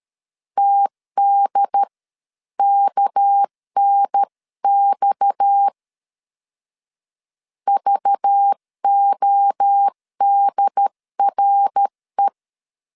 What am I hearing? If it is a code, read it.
Morse code, heard as a keyed tone, about 13 wpm, spaced as for 11 wpm, 800 Hz. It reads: TDKNX VODRE
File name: CW / DET / 1 6 1500C S C 12wpm 3533